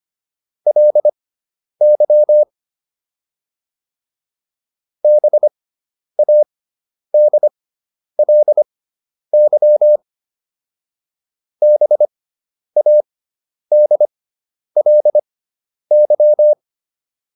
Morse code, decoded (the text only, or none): LY BADLY BADLY